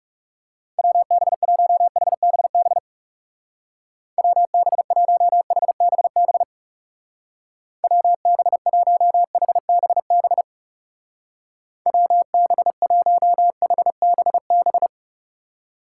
Morse code, heard as a keyed tone, about 39 wpm, 700 Hz. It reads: WB1HBB WB1HBB WB1HBB WB1HBB